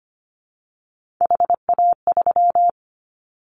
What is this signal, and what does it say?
Morse code, heard as a keyed tone, about 25 wpm, 700 Hz. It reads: HA3
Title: Callsign Prefixes for Top 25 DXCC - Repeated 3x - 25wpm